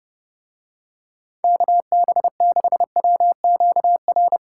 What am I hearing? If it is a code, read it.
Morse code, heard as a keyed tone, about 28 wpm, 700 Hz. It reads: KB6WQR